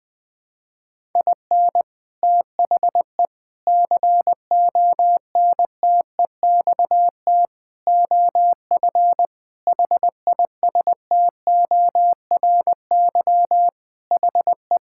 Morse code, heard as a keyed tone, about 19 wpm, 700 Hz. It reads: IN THE CONTEXT OF HISTORY HE
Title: Sentences with 4-Word Phrases - 20wpm